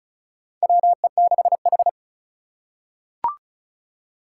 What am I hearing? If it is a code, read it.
Morse code, heard as a keyed tone, about 34 wpm, 700 Hz. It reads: WE6H
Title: USA Call Signs - 4 Characters - 3x Character Spacing - 35wpm